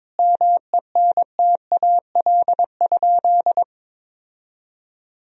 Morse code, heard as a keyed tone, about 21 wpm, 700 Hz. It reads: MENTAL?
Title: Sentences from Top 1000 Words - Review Part 4 - 22wpm